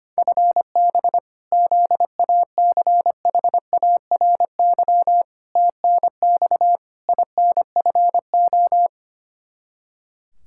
Morse code, heard as a keyed tone, about 25 wpm, 700 Hz. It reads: FB ZACHARY TNX INFO
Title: Practice QSO - Exchange 3 - More Info - 25wpm